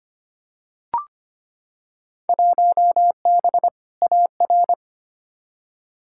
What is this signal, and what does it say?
Morse code, heard as a keyed tone, about 25 wpm, 700 Hz. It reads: 1B AR
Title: Field Day Exchanges Encore - 25wpm